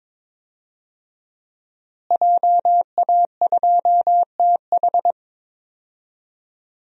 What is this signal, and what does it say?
Morse code, heard as a keyed tone, about 22 wpm, 700 Hz. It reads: JA2TH
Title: DXCC-100 Call Signs - 5 Characters - Repeated 3x - 22wpm